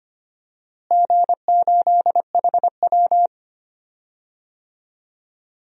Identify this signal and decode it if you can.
Morse code, heard as a keyed tone, about 25 wpm, 700 Hz. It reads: G8HW
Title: DXCC-100 Call Signs - 4 Characters - Repeated 2x - 25wpm